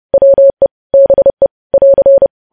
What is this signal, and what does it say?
Morse code, heard as a keyed tone, about 30 wpm, 550 Hz. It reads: WE BE <AR>
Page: news headlines in morse code at 30 wpm